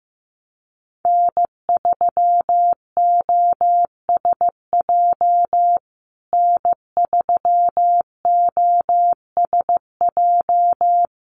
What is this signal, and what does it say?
Morse code, heard as a keyed tone, about 15 wpm, 700 Hz. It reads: N3OSJ N3OSJ
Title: USA Call Signs Repeated 2x 15wpm